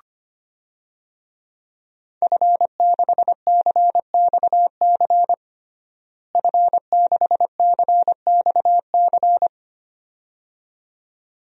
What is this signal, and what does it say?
Morse code, heard as a keyed tone, about 25 wpm, 700 Hz. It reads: F6CXC F6CXC
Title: DXCC-100 Call Signs - 5 Characters - Repeated 2x - 25wpm